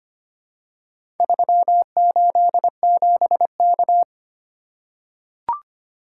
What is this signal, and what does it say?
Morse code, heard as a keyed tone, about 25 wpm, 700 Hz. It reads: 387K
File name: practice / 25wpm / Four Letter-Number Combinations - 25wpm